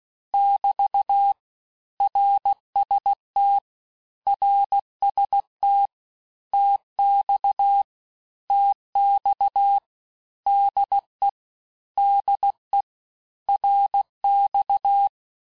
Morse code, heard as a keyed tone, about 16 wpm, 800 Hz. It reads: <BT> RST RST TX TX DE DE RX
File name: cours-du-lundi files / Cours-1586-texte-abreviations-vitesse-15mpm-puis-20mpm